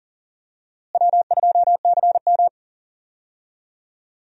Morse code, heard as a keyed tone, about 37 wpm, 700 Hz. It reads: W2CM
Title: USA Call Signs - 4 Characters - Repeated 2x - 40wpm